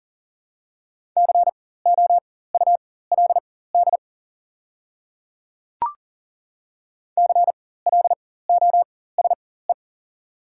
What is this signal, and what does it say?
Morse code, heard as a keyed tone, about 40 wpm, spaced as for 19 wpm, 700 Hz. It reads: COULD CLOSE
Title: 4x Character Spacing - Top 300 Words - 40wpm